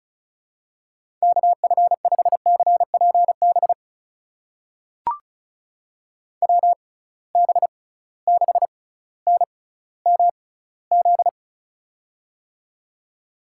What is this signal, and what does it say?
Morse code, heard as a keyed tone, about 36 wpm, spaced as for 18 wpm, 700 Hz. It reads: KF5CPB WB6NMZ
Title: USA Call Signs - 6 Characters - 6x Character Spacing - 35wpm